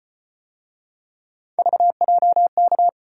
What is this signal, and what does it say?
Morse code, heard as a keyed tone, about 34 wpm, 700 Hz. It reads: VJK